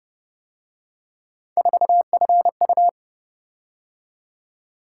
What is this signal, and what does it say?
Morse code, heard as a keyed tone, about 29 wpm, 700 Hz. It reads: 4FU
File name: Speed-Racing - Three Letter-Number Combinations - 25wpm